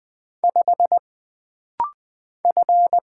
Morse code, heard as a keyed tone, about 20 wpm, 700 Hz. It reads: 5 F
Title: Single Letter-Number Alt - Rapid-Fire 20wpm